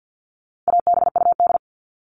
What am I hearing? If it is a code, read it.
Morse code, heard as a keyed tone, about 50 wpm, 700 Hz. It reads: ABUD
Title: All Cut Numbers - 4-Digit 50wpm